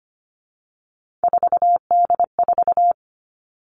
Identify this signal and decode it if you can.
Morse code, heard as a keyed tone, about 25 wpm, 700 Hz. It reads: 4D4
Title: All Cut Numbers - 3-Digit 25wpm